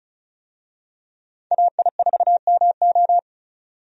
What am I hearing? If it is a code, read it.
Morse code, heard as a keyed tone, about 34 wpm, 700 Hz. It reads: AI4MO
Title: USA Call Signs - 5 Characters - 35wpm